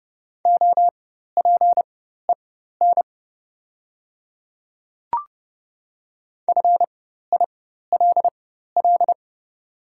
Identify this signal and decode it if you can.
Morse code, heard as a keyed tone, about 30 wpm, spaced as for 14 wpm, 700 Hz. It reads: OPEN FILL
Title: 4x Character Spacing - Top 500 Words - 30wpm